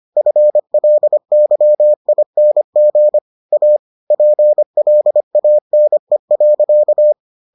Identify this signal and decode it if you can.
Morse code, heard as a keyed tone, about 25 wpm, 600 Hz. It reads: FLYING A PLANE.